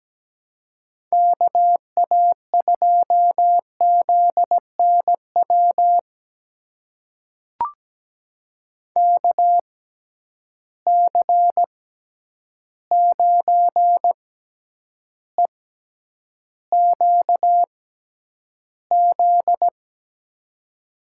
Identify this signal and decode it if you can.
Morse code, heard as a keyed tone, about 17 wpm, spaced as for 9 wpm, 700 Hz. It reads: KA2ZNW KC9EQZ
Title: USA Call Signs - 6 Characters - 6x Character Spacing - 17wpm